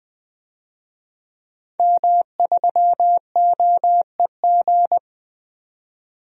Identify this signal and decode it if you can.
Morse code, heard as a keyed tone, about 20 wpm, 700 Hz. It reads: M3OEG
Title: DXCC-100 Call Signs - 5 Characters - 20wpm